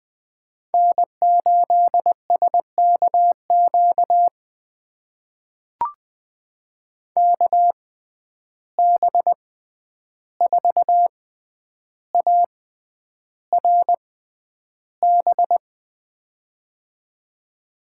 Morse code, heard as a keyed tone, about 20 wpm, spaced as for 10 wpm, 700 Hz. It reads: N8SKQ KB4ARB
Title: USA Call Signs - 6x Character Spacing - 20wpm